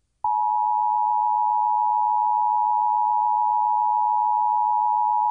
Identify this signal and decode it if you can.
Emergency Alert System (SAME) tones (the 853 and 960 Hz attention tone). It is heard from 0.8 to 4.9 s.